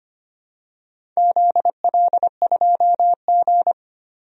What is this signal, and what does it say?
Morse code, heard as a keyed tone, about 25 wpm, 700 Hz. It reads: ZL2G
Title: DXCC-100 Call Signs - 4 Characters - Repeated 3x - 25wpm